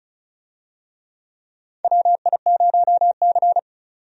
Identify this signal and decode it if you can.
Morse code, heard as a keyed tone, about 36 wpm, 700 Hz. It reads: WI0C
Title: USA Call Signs - 4 Characters - 35wpm